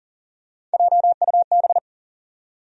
Morse code, heard as a keyed tone, about 40 wpm, 700 Hz. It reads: JUB
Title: USA Call Sign Postfixes - 3 Letters 40wpm